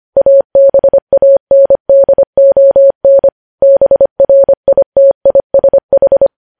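Morse code, heard as a keyed tone, about 25 wpm, 550 Hz. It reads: ABANDON BRITISH
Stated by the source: Morse practice of news headlines at 25wpm.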